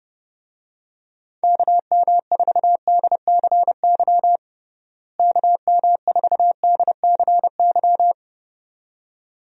Morse code, heard as a keyed tone, about 31 wpm, 700 Hz. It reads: KM4DCY KM4DCY
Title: USA Call Signs - 6 Characters - Repeated 2x - 30wpm